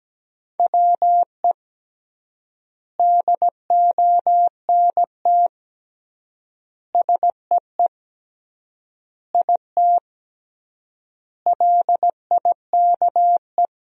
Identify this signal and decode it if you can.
Morse code, heard as a keyed tone, about 17 wpm, 700 Hz. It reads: WE DONT SEE IT LIKE
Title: 3x Word-Spacing - Sentences from Top 300 Words - 17wpm